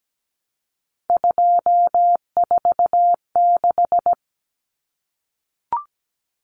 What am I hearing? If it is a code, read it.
Morse code, heard as a keyed tone, about 17 wpm, 700 Hz. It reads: 246
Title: Numbers - 3-Digit 17wpm